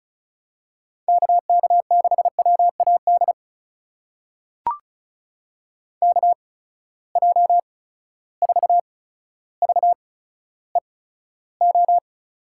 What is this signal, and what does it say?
Morse code, heard as a keyed tone, about 36 wpm, spaced as for 15 wpm, 700 Hz. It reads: KK6WAD KJ4VEO
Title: USA Call Signs - 6 Characters - 8x Character Spacing - 35wpm